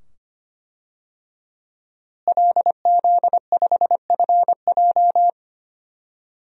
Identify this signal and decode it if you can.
Morse code, heard as a keyed tone, about 25 wpm, 700 Hz. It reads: LZ5FJ
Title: DXCC-100 Call Signs - 5 Characters - Repeated 2x - 25wpm